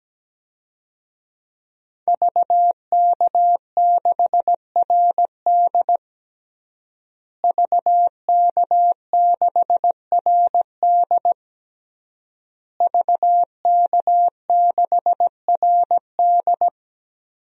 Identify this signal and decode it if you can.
Morse code, heard as a keyed tone, about 17 wpm, 700 Hz. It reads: VK6RD VK6RD VK6RD
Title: DXCC-100 Call Signs - 5 Characters - Repeated 3x - 17wpm